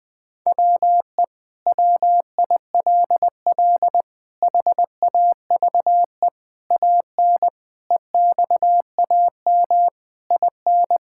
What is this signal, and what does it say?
Morse code, heard as a keyed tone, about 20 wpm, 700 Hz. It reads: WE WILL HAVE AN EXAM IN
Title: Sentences from Top 1000 Words - Review Part 5 - 20wpm